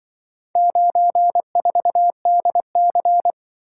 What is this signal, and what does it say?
Morse code, heard as a keyed tone, about 25 wpm, 700 Hz. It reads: 94DC
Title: Speed-Racing - Four Letter-Number Combinations - 20wpm